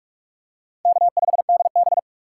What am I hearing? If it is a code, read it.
Morse code, heard as a keyed tone, about 46 wpm, 700 Hz. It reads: K5DB
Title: USA Call Signs - 4 Characters - 6x Character Spacing - 45wpm